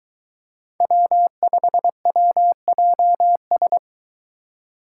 Morse code, heard as a keyed tone, about 23 wpm, 700 Hz. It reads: W5WJS